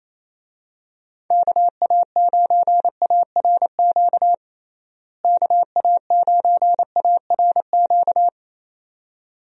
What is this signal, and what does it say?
Morse code, heard as a keyed tone, about 28 wpm, 700 Hz. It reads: KA9ARQ KA9ARQ
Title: USA Call Signs - 6 Characters - Repeated 2x - 28wpm